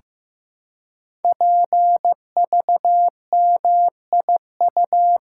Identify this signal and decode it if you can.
Morse code, heard as a keyed tone, about 15 wpm, 700 Hz. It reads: PVMIU
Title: Code Groups - 5 Letter:Numbers - 15wpm